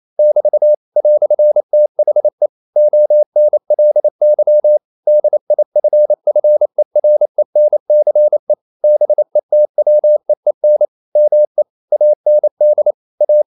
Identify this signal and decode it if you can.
Morse code, heard as a keyed tone, about 28 wpm, 600 Hz. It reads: <BT> "THE ONLY DIFFERENCE BETWEEN ME AND A
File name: CW / Practice / EnglishQuotations0128